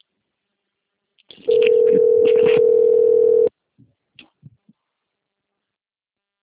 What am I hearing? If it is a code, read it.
Ringback tone (North America), 440 + 480 Hz.